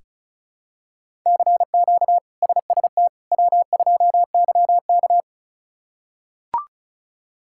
Morse code, heard as a keyed tone, about 36 wpm, 700 Hz. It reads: CQ SST W2YK
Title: K1USN SST QSO Practice - 35wpm